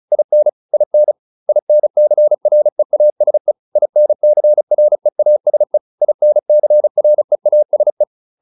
Morse code, heard as a keyed tone, about 34 wpm, 600 Hz. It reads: IN IN INCREASE INCREASE INCREASE